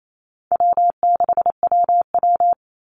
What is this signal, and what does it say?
Morse code, heard as a keyed tone, about 28 wpm, 700 Hz. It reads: W6WW
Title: NA SOTA Chaser Callsigns 28wpm